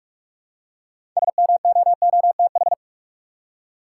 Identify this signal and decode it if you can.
Morse code, heard as a keyed tone, about 43 wpm, 700 Hz. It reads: SMOOTH
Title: Six Letter Words - 45wpm